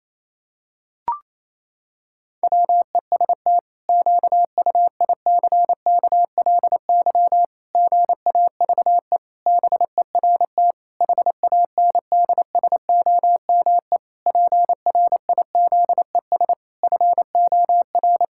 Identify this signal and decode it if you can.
Morse code, heard as a keyed tone, about 28 wpm, 700 Hz. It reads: WEST QUICKLY GAVE BERT HANDSOME PRIZES FOR
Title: Pangram Challenge v2 - 28wpm